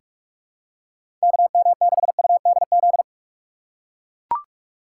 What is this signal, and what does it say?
Morse code, heard as a keyed tone, about 44 wpm, 700 Hz. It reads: KM6UDZ
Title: USA Call Signs - 6 Characters - 6x Character Spacing - 45wpm